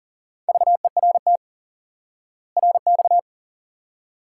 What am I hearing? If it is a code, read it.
Morse code, heard as a keyed tone, about 39 wpm, 700 Hz. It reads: VERT RX
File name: IQE Course 3 - Lesson 89 - YAGI - Two Elements - 40wpm